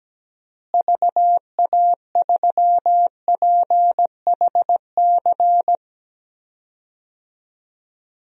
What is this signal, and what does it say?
Morse code, heard as a keyed tone, about 17 wpm, 700 Hz. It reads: VA3PHC
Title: DXCC-100 Call Signs - 6 Characters - Repeated 3x - 17wpm